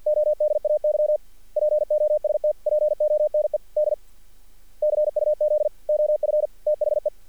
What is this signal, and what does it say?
Morse code, heard as a keyed tone, about 48 wpm, 600 Hz. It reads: ODAY POSTPONED CUZ OF THE
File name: archived / CRSnet